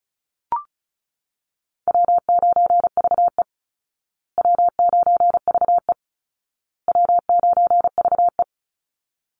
Morse code, heard as a keyed tone, about 34 wpm, 700 Hz. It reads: W9VE W9VE W9VE
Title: Easy Call Signs Repeated 3x - 35wpm